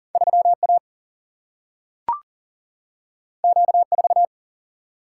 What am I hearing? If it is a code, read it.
Morse code, heard as a keyed tone, about 40 wpm, 700 Hz. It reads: 3 A Q 4
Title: Single-Letter Single-Number 40wpm